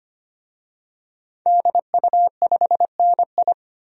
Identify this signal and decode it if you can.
Morse code, heard as a keyed tone, about 25 wpm, 700 Hz. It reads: DU5NI
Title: DXCC-100 Call Signs - 5 Characters - 25wpm